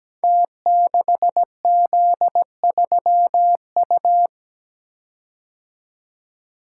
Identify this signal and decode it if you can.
Morse code, heard as a keyed tone, about 17 wpm, 700 Hz. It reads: T6Z3U
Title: Code Groups - 5 Letter:Numbers - 17wpm